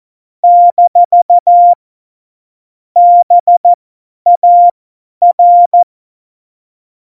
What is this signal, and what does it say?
Morse code, heard as a keyed tone, about 14 wpm, spaced as for 10 wpm, 700 Hz. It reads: - BAR
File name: Morsekurs / Witze / Witz 25. September 2018, 50 F70